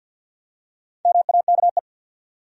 Morse code, heard as a keyed tone, about 53 wpm, 700 Hz. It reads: MAKE
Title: Top 100 Words 50wpm